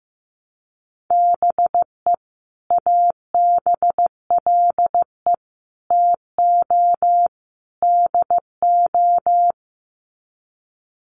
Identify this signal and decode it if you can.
Morse code, heard as a keyed tone, about 15 wpm, 700 Hz. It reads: BE ABLE TO DO
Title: Sets of 4 Words 15wpm